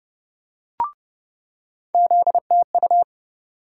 Morse code, heard as a keyed tone, about 31 wpm, 700 Hz. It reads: ZTU
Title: USA Call Sign Postfixes - 3 Letters 30wpm